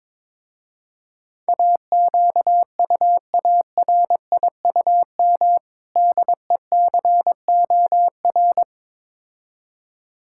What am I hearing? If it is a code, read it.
Morse code, heard as a keyed tone, about 22 wpm, 700 Hz. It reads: AQUARIUM DECOR
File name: Difficult Words with Hints - 22wpm